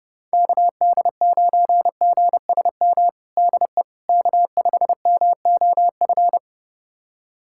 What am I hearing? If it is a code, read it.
Morse code, heard as a keyed tone, about 30 wpm, 700 Hz. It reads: KD9GSM DE K5MOF
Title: USA Callsign DE Callsign 30wpm